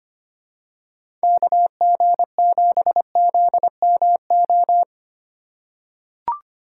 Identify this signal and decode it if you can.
Morse code, heard as a keyed tone, about 25 wpm, 700 Hz. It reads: KG7ZMO
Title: USA Call Signs - 6 Characters - 3x Character Spacing - 25wpm